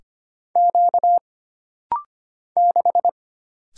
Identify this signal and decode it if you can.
Morse code, heard as a keyed tone, about 25 wpm, 700 Hz. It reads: Q 6
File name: Single Character (Letter:Number:Punctuation:Prosign) - Rapid-Fire - 25wpm